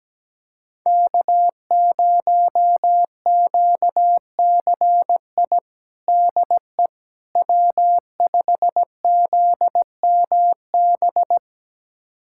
Morse code, heard as a keyed tone, about 17 wpm, 700 Hz. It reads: K0QCI DE W5ZMB
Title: USA Callsign DE Callsign - 5 Characters - 17wpm